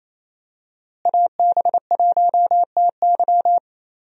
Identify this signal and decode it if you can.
Morse code, heard as a keyed tone, about 28 wpm, 700 Hz. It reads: AB1TY
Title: USA Call Signs - 5 Characters - Repeated 2x - 28wpm